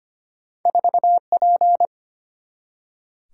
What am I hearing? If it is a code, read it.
Morse code, heard as a keyed tone, about 25 wpm, 700 Hz. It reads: 4P
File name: Two Letter-Number Combinations - 25wpm